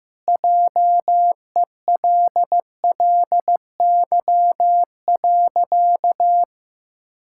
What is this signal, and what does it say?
Morse code, heard as a keyed tone, about 15 wpm, 700 Hz. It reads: JELLY.
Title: Pangram Challenge v2 - 15wpm